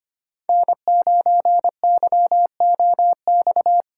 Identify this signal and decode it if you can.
Morse code, heard as a keyed tone, about 25 wpm, 700 Hz. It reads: N9YOX
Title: USA Call Signs - 5 Characters - 2x Character Spacing - 25wpm